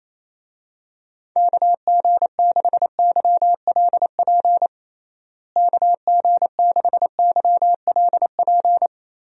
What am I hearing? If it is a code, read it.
Morse code, heard as a keyed tone, about 28 wpm, 700 Hz. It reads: KG6YLP KG6YLP
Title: USA Call Signs - 6 Characters - Repeated 3x - 28wpm